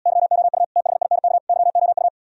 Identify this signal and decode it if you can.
Morse code, heard as a keyed tone, about 80 wpm, 690 Hz. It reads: YOU THINK YOU